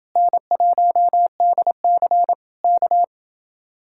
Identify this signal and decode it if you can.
Morse code, heard as a keyed tone, about 29 wpm, 700 Hz. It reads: N1DC K